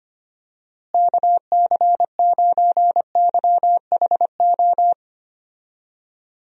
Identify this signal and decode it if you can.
Morse code, heard as a keyed tone, about 25 wpm, 700 Hz. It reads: KC9YHO
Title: USA Call Signs - 6 Characters - Repeated 2x - 25wpm